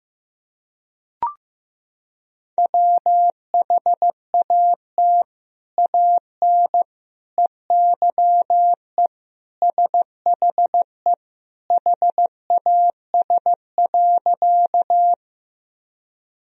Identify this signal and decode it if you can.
Morse code, heard as a keyed tone, about 15 wpm, 700 Hz. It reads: WHAT AN EYE SHE HAS.
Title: Sentences from Top 1000 Words - Review Part 1 - 15wpm